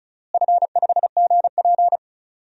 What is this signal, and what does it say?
Morse code, heard as a keyed tone, about 34 wpm, 700 Hz. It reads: F5GP
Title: DXCC-100 Call Signs - 4 Characters - Repeated 2x - 35wpm